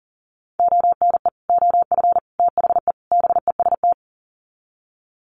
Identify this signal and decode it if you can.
Morse code, heard as a keyed tone, about 40 wpm, 700 Hz. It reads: ONE OF THE BEST
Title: Sets of 4 Words 40wpm